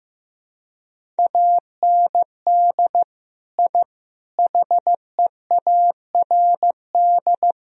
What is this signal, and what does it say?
Morse code, heard as a keyed tone, about 15 wpm, 700 Hz. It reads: AND I HEARD
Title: Sentences from Top 1000 Words - Review Part 1 - 15wpm